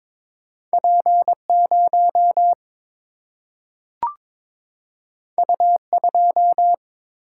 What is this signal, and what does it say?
Morse code, heard as a keyed tone, about 21 wpm, 700 Hz. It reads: P0 U2
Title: Single-Letter Single-Number 22wpm